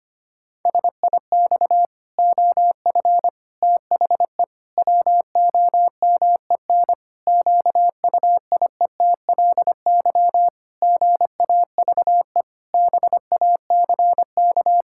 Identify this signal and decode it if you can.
Morse code, heard as a keyed tone, about 25 wpm, 700 Hz. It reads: SIX OF THE WOMEN QUIETLY GAVE BACK
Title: Pangram Challenge v2 - 25wpm